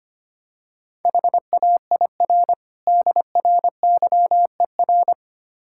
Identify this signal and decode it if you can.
Morse code, heard as a keyed tone, about 25 wpm, 700 Hz. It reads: HAIR DRYER
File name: Easy Words - Speed-Racing with Hints - 25wpm